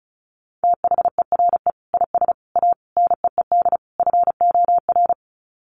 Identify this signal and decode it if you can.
Morse code, heard as a keyed tone, about 34 wpm, 700 Hz. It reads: THERE IS A NEED FOR
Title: Sets of 5 Words Encore2 35wpm